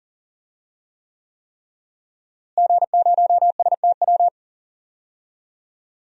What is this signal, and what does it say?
Morse code, heard as a keyed tone, about 40 wpm, 700 Hz. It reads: G0STW
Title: DXCC-100 Call Signs - 5 Characters - Repeated 3x - 40wpm